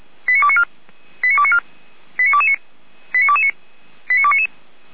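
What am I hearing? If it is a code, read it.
Five-tone selective call (ZVEI), heard as quick runs of five tones, tones of about 70 ms. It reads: 89294 89294 89229 89229 89299
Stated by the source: Received on an SDR, from the feuerwehr-orlamuende.org (fire brigade Orlamünde) server.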